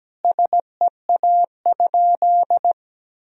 Morse code, heard as a keyed tone, about 17 wpm, 700 Hz. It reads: SEA?
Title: Sentences from Top 1000 Words - Review Part 1 - 17wpm